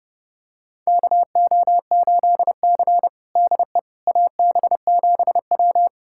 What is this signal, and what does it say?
Morse code, heard as a keyed tone, about 30 wpm, 700 Hz. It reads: KO8C DE AB7W
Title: USA Callsign DE Callsign - 4 Characters - 30wpm